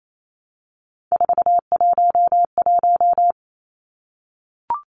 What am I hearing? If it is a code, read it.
Morse code, heard as a keyed tone, about 28 wpm, 700 Hz. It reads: 411
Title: Numbers - 3-Digit 28wpm